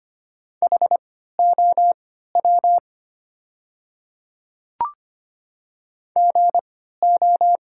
Morse code, heard as a keyed tone, about 25 wpm, spaced as for 14 wpm, 700 Hz. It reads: HOW GO
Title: 3x Character Spacing - Top 100 Words - 25wpm